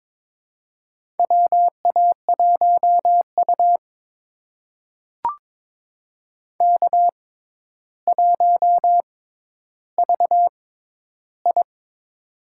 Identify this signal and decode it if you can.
Morse code, heard as a keyed tone, about 22 wpm, spaced as for 11 wpm, 700 Hz. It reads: WA1U K1VI
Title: USA Call Signs - 4 Characters - 6x Character Spacing - 22wpm